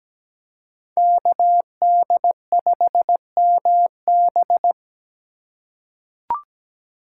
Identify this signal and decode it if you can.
Morse code, heard as a keyed tone, about 17 wpm, 700 Hz. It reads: KD5MB
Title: USA Call Signs - 5 Characters - 4x Character Spacing - 17wpm